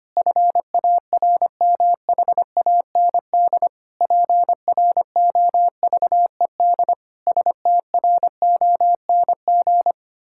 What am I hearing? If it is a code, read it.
Morse code, heard as a keyed tone, about 25 wpm, 700 Hz. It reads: FARMHAND PROVED STRONG
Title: Pangram Challenge v2 - 25wpm